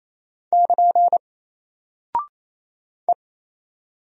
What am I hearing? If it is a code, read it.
Morse code, heard as a keyed tone, about 28 wpm, 700 Hz. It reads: <KN> E
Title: Single Character (Letter:Number:Punctuation:Prosign) Alt 28wpm